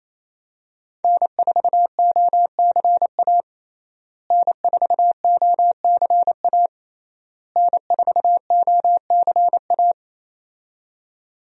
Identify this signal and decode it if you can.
Morse code, heard as a keyed tone, about 27 wpm, 700 Hz. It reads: N4OCA N4OCA N4OCA
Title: USA Call Signs - 5 Characters - Repeated 3x - 28wpm